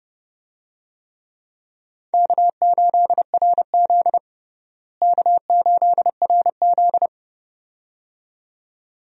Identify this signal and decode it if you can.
Morse code, heard as a keyed tone, about 30 wpm, 700 Hz. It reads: K8RZ K8RZ
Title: USA Call Signs - 4 Characters - Repeated 2x - 30wpm